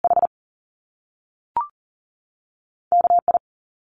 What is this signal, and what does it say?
Morse code, heard as a keyed tone, about 40 wpm, 700 Hz. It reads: H KI
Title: Top Two Letter Combinations in Words 40wpm